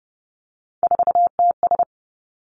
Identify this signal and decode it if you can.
Morse code, heard as a keyed tone, about 31 wpm, 700 Hz. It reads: 4TS
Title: Three Challenging Letter-Number Combinations 30wpm